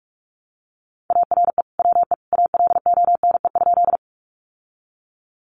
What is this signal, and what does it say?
Morse code, heard as a keyed tone, about 44 wpm, 700 Hz. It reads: ARE WE ALONE?
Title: Sentences from Top 1000 Words 45wpm